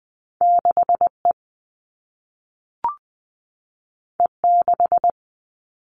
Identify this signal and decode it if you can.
Morse code, heard as a keyed tone, about 20 wpm, 700 Hz. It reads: 6E E6
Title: All Cut Numbers - 2-Digit 20wpm